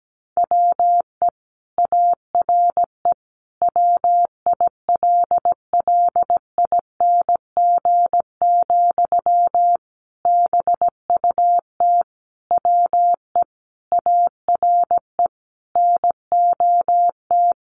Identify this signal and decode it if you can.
Morse code, heard as a keyed tone, about 17 wpm, 700 Hz. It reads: WE ARE WILLING, BUT WE ARE NOT
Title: Sentences from Top 1300 Words 17wpm